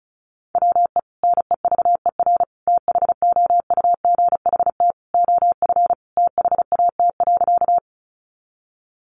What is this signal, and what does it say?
Morse code, heard as a keyed tone, about 34 wpm, 700 Hz. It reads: WE NEVER THOUGHT OF THAT.
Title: Sentences from Top 300 Words 35wpm